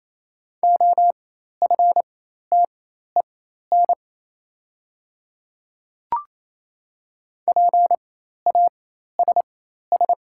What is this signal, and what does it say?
Morse code, heard as a keyed tone, about 28 wpm, spaced as for 13 wpm, 700 Hz. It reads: OFTEN PASS
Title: 4x Character Spacing - Top 500 Words - 28wpm